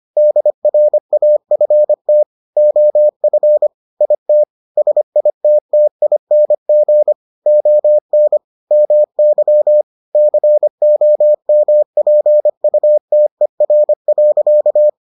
Morse code, heard as a keyed tone, about 25 wpm, 600 Hz. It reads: DRAFT OF IT SITTING ON MY COMPUTER.